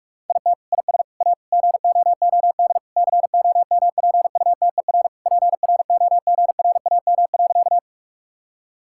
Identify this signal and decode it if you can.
Morse code, heard as a keyed tone, about 41 wpm, 700 Hz. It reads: IT IS A GOOD COMPUTER PROGRAM.
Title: Sentences from Top 1000 Words - Review Part 5 - 45wpm